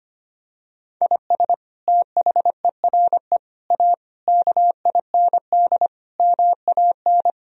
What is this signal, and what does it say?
Morse code, heard as a keyed tone, about 25 wpm, 700 Hz. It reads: IS THERE A KIND MAN
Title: Sentences from Top 1000 Words - Review Part 3 - 25wpm